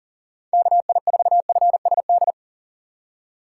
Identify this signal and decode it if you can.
Morse code, heard as a keyed tone, about 39 wpm, 700 Hz. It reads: KI4FSD
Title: USA Call Signs - 6 Characters - 4x Character Spacing - 40wpm